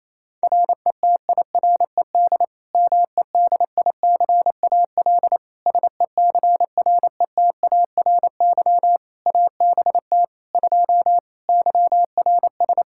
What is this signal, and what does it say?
Morse code, heard as a keyed tone, about 28 wpm, 700 Hz. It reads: RETIRED MEDICAL SECRETARY ABT 2 YRS
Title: Practice QSO - Exchange 3 - More Info - 28wpm